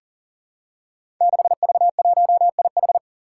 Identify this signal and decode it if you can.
Morse code, heard as a keyed tone, about 40 wpm, 700 Hz. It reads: 6V1IH
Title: Five Challenging Letter-Number Combinations 40wpm